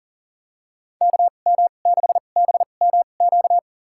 Morse code, heard as a keyed tone, about 40 wpm, spaced as for 29 wpm, 700 Hz. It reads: KM6BMQ
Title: USA Call Signs - 6 Characters - 2x Character Spacing - 40wpm